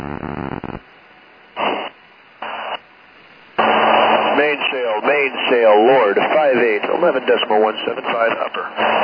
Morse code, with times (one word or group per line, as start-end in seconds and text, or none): none